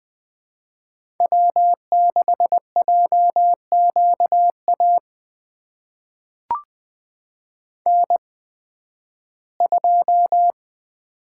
Morse code, heard as a keyed tone, about 21 wpm, spaced as for 13 wpm, 700 Hz. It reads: W6JQA N2
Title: USA Call Signs - 5 Characters - 8x Character Spacing - 20wpm